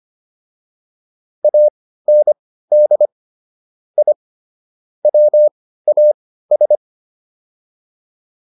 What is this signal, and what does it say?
Morse code, heard as a keyed tone, about 25 wpm, spaced as for 15 wpm, 600 Hz. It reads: AND I WAS